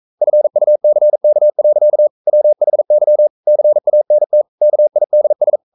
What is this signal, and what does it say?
Morse code, heard as a keyed tone, about 42 wpm, 600 Hz. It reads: FUCK. WHY CANT KIDS
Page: Romanen Never af Ken Follet på engelsk.